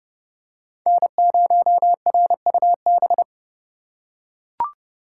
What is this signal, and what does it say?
Morse code, heard as a keyed tone, about 30 wpm, 700 Hz. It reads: N0RUB